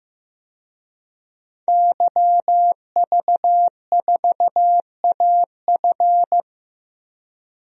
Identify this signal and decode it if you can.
Morse code, heard as a keyed tone, about 15 wpm, 700 Hz. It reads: YV4AF
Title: DXCC-100 Call Signs - 5 Characters - 15wpm